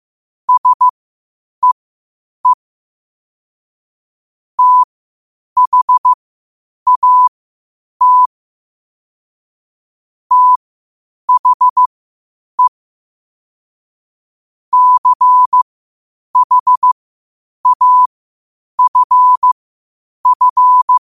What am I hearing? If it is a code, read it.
Morse code, heard as a keyed tone, about 15 wpm, spaced as for 8 wpm, 1000 Hz. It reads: SEE THAT THE CHAFF
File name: qotdfiles / 8wpm